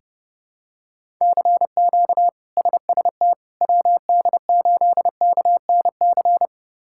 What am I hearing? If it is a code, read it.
Morse code, heard as a keyed tone, about 30 wpm, 700 Hz. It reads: CQ SST WD8KNC
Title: K1USN SST QSO Practice - 30wpm